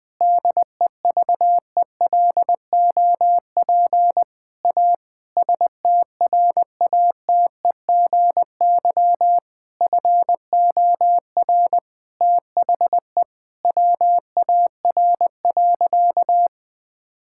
Morse code, heard as a keyed tone, about 20 wpm, 700 Hz. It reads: DEVELOP A STRATEGY FOR THE WAR.
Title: Sentences from Top 1000 Words - Review Part 5 - 20wpm